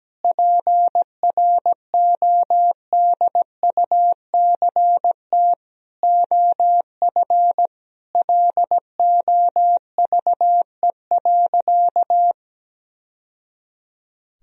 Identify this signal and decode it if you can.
Morse code, heard as a keyed tone, about 17 wpm, 700 Hz. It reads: PRODUCT OF LOVE.
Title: Sentences from Top 1000 Words - Review Part 2 - 17wpm